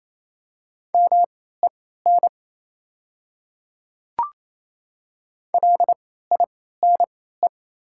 Morse code, heard as a keyed tone, about 28 wpm, spaced as for 16 wpm, 690 Hz. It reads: MEN LINE
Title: 3x Character Spacing - Top 200 Words - 28wpm